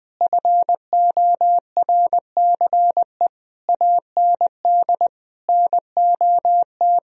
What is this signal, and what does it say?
Morse code, heard as a keyed tone, about 20 wpm, 700 Hz. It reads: FORCE AND NOT
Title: Sentences from Top 1000 Words - Review Part 3 - 20wpm